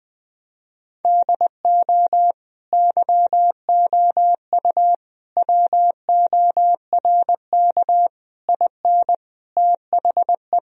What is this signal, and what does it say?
Morse code, heard as a keyed tone, about 21 wpm, 700 Hz. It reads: DO YOU WORK IN THE
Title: Sentences from Top 1000 Words - Review Part 4 - 20wpm